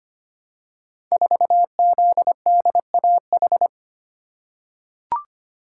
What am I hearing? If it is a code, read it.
Morse code, heard as a keyed tone, about 25 wpm, 700 Hz. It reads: 4ZDAH
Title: Code Groups - 5 Letter:Numbers - 25wpm